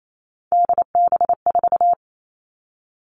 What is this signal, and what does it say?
Morse code, heard as a keyed tone, about 28 wpm, 700 Hz. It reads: DB4
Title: All Cut Numbers - 3-Digit 28wpm